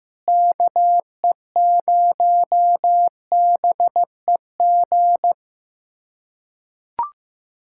Morse code, heard as a keyed tone, about 15 wpm, 700 Hz. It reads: KE0BEG
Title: USA Callsign DE Callsign 15wpm